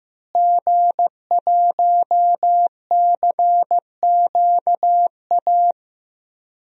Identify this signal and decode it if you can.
Morse code, heard as a keyed tone, about 15 wpm, 700 Hz. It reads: G1CQA
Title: DXCC-100 Call Signs - 5 Characters - Repeated 3x - 15wpm